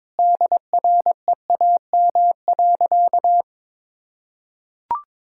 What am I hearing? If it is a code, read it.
Morse code, heard as a keyed tone, about 22 wpm, 700 Hz. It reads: DREAM.
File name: practice / 22wpm / Sentences with 2-Word Phrases - 22wpm